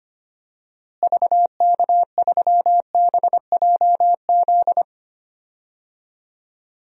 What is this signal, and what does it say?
Morse code, heard as a keyed tone, about 25 wpm, 700 Hz. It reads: VK3BJZ
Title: DXCC-100 Call Signs - 6 Characters - Repeated 3x - 25wpm